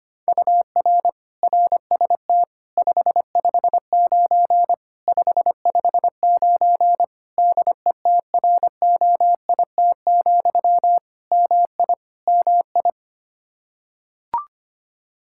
Morse code, heard as a keyed tone, about 25 wpm, 700 Hz. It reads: UR RST 559 559 DETROIT, MI MI
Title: Practice QSO - Exchange 1 - Beginning - 25wpm